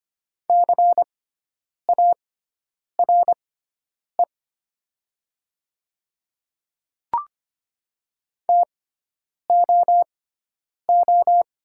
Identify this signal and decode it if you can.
Morse code, heard as a keyed tone, about 25 wpm, spaced as for 9 wpm, 700 Hz. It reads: CARE TOO
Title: 6x Character Spacing - Top 300 Words - 25wpm